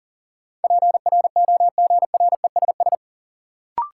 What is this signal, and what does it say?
Morse code, heard as a keyed tone, about 40 wpm, 700 Hz. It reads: PROGRESS
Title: Sentences from Top 1400 Words - New Word Drill Alt - 40wpm